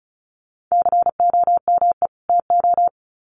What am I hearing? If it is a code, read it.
Morse code, heard as a keyed tone, about 34 wpm, 700 Hz. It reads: COME TO
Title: Sets of 2 Words 35wpm